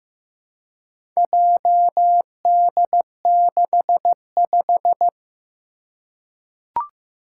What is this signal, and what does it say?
Morse code, heard as a keyed tone, about 15 wpm, 700 Hz. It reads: JD65
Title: Four Challenging Letter-Number Combinations 15wpm